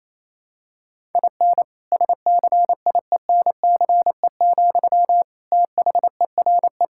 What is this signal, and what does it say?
Morse code, heard as a keyed tone, about 27 wpm, 700 Hz. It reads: IN SCIENCE, THERE
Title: Sentences from Top 1000 Words - Review Part 5 - 28wpm